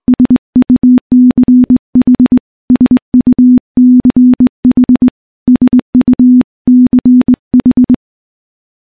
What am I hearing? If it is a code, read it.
Morse code, heard as a keyed tone, about 20 wpm, 250 Hz. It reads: SUCH SUCH SUCH
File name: morse / morse-training-20WPM